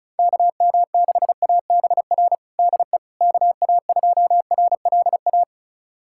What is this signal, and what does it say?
Morse code, heard as a keyed tone, about 36 wpm, 700 Hz. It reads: KM6ABR DE KA2RLA
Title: USA Callsign DE Callsign 35wpm